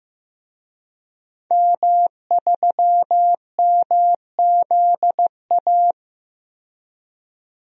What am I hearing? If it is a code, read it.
Morse code, heard as a keyed tone, about 16 wpm, 700 Hz. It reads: M3MZA
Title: DXCC-100 Call Signs - 5 Characters - 15wpm